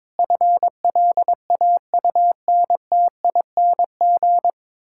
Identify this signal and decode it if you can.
Morse code, heard as a keyed tone, about 22 wpm, 700 Hz. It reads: FLAUNTING
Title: Pangram Challenge v2 - 22wpm